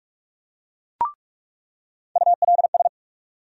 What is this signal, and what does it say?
Morse code, heard as a keyed tone, about 47 wpm, 700 Hz. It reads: ULS
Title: USA Call Sign Postfixes - 3 Letters 45wpm